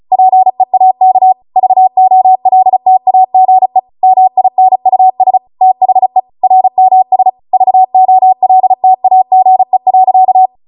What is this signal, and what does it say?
Morse code, heard as a keyed tone, about 36 wpm, 750 Hz. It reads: PEAK VOLTAGE MINUS THE RMS VOLTAGE.